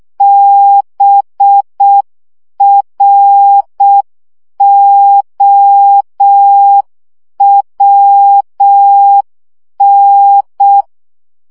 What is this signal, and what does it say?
Morse code, heard as a keyed tone, about 6 wpm, 800 Hz. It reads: BROWN
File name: CW 5 baud